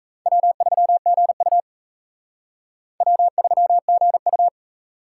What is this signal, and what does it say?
Morse code, heard as a keyed tone, about 34 wpm, 700 Hz. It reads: W3GU W3GU
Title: Speed Racing - USA 1x2 Call Signs 28wpm